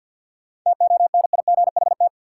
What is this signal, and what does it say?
Morse code, heard as a keyed tone, about 53 wpm, 700 Hz. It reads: TONIGHT
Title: Sentences from Top 600 Words - New Word Drill - 50wpm